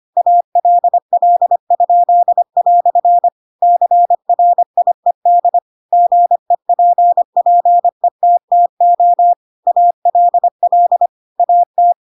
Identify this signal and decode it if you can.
Morse code, heard as a keyed tone, about 25 wpm, 700 Hz. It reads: ALL?" CRIED GEPPETTO ALL AT
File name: Pinocchio / Pinocchio ch8 25 25 ik1hge